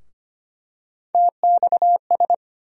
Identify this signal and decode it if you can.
Morse code, heard as a keyed tone, about 25 wpm, 700 Hz. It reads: TXS
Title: USA Call Sign Postfixes - 3 Letters 25wpm